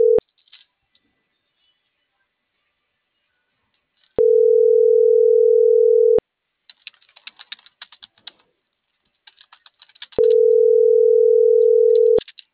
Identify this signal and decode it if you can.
Ringback tone (North America), 440 + 480 Hz.